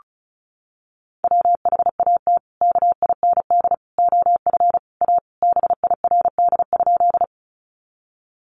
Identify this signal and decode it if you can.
Morse code, heard as a keyed tone, about 34 wpm, 700 Hz. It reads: WHAT KIND OF A BIRD?
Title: Sentences from Top 400 Words 35wpm